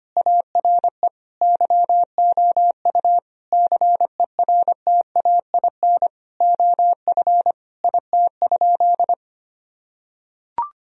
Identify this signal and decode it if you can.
Morse code, heard as a keyed tone, about 25 wpm, 700 Hz. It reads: ARE YOU CERTAIN OF IT?
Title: Sentences from Top 1000 Words - Review Part 3 - 25wpm